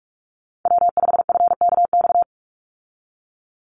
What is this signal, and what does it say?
Morse code, heard as a keyed tone, about 44 wpm, 700 Hz. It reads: W5FKX
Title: Call Signs - 45wpm